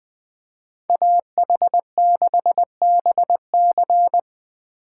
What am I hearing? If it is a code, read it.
Morse code, heard as a keyed tone, about 20 wpm, 700 Hz. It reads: AH6BC
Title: USA Call Signs - 5 Characters - 8x Character Spacing - 20wpm